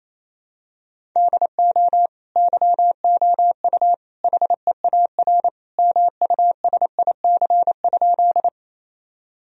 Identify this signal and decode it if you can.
Morse code, heard as a keyed tone, about 27 wpm, 700 Hz. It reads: DO YOU HEAR MUSIC?
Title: Sentences from Top 1000 Words - Review Part 1 - 28wpm